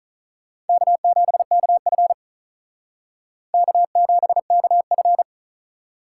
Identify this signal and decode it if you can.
Morse code, heard as a keyed tone, about 44 wpm, 700 Hz. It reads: K7KF K7KF